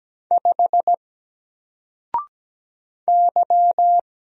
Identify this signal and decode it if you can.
Morse code, heard as a keyed tone, about 17 wpm, 700 Hz. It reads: 5 Y
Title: Single Letter-Number Alt 17wpm